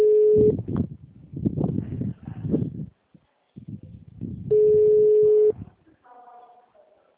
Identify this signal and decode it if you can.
Ringback tone (Europe, 425 Hz).
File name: out- -2008- - - .48954